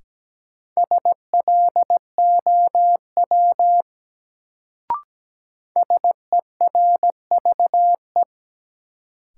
Morse code, heard as a keyed tone, about 17 wpm, 700 Hz. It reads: SLOW SERVE
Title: Sentences from Top 400 Words - New Word Drill - 17wpm